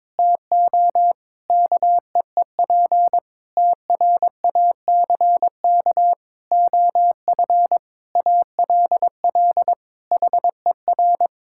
Morse code, heard as a keyed tone, about 22 wpm, 700 Hz. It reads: TO KEEP TRACK OF ALL HER
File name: Difficult Sentences with Hints - 22wpm